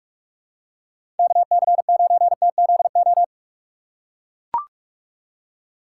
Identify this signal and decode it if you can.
Morse code, heard as a keyed tone, about 44 wpm, 700 Hz. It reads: KC9TZO
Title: K1USN SST QSO Practice - 45wpm